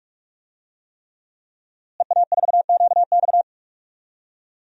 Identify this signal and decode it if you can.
Morse code, heard as a keyed tone, about 48 wpm, 700 Hz. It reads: EA4QX
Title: DXCC-100 Call Signs - 5 Characters - Repeated 3x - 45wpm